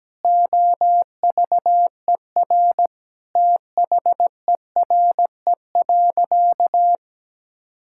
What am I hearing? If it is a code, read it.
Morse code, heard as a keyed tone, about 17 wpm, 700 Hz. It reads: OVER THERE.
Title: Sentences from Top 1600 Words 17wpm